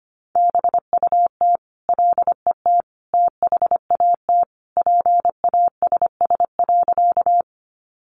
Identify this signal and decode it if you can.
Morse code, heard as a keyed tone, about 25 wpm, 700 Hz. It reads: BUT LET THAT PASS.
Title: Sentences from Top 400 Words 25wpm